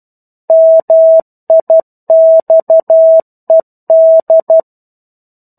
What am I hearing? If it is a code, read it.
Morse code, heard as a keyed tone, about 12 wpm, 650 Hz. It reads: MIXED
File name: 01-Chapter-01 12 WPM